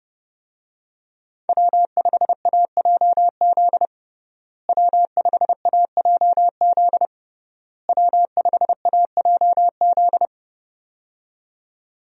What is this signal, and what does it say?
Morse code, heard as a keyed tone, about 30 wpm, 700 Hz. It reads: W5AJZ W5AJZ W5AJZ
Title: USA Call Signs - 5 Characters - Repeated 3x - 30wpm